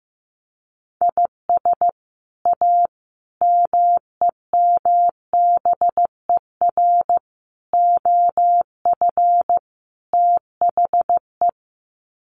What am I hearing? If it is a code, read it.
Morse code, heard as a keyed tone, about 15 wpm, 700 Hz. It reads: IS A MEMBER OF THE
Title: Sets of 5 Words 15wpm